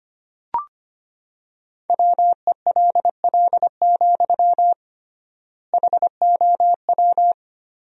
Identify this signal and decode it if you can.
Morse code, heard as a keyed tone, about 25 wpm, 700 Hz. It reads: WELL, HOW
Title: 3x Word-Spacing - Sentences from Top 200 Words - 25wpm